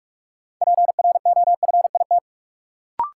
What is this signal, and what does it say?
Morse code, heard as a keyed tone, about 42 wpm, 700 Hz. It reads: PROFIT
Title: Sentences from Top 800 Words - New Word Drill - 45wpm